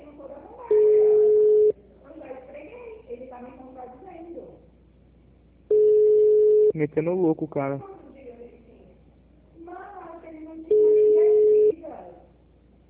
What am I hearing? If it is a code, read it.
Ringback tone (Europe, 425 Hz), with speech.